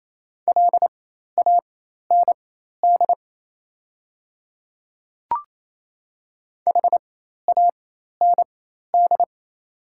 Morse code, heard as a keyed tone, about 27 wpm, spaced as for 13 wpm, 690 Hz. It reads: LAND HAND
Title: 4x Character Spacing - Top 200 Words - 28wpm